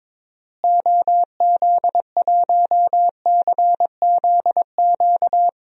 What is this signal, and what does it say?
Morse code, heard as a keyed tone, about 22 wpm, 700 Hz. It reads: OZ1CZQ
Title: DXCC-100 Call Signs - 6 Characters - Repeated 2x - 22wpm